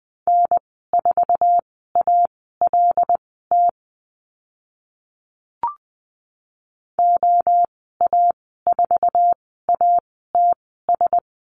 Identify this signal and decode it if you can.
Morse code, heard as a keyed tone, about 20 wpm, spaced as for 14 wpm, 700 Hz. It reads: N4ALT OA4ATS
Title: 076 Lesson 13g - Letter L - Call Signs 20wpm